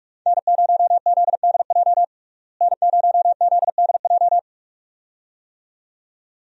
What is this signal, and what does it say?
Morse code, heard as a keyed tone, about 46 wpm, 700 Hz. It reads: N0ZDJ N0ZDJ
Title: USA Call Signs - 5 Characters - Repeated 2x - 45wpm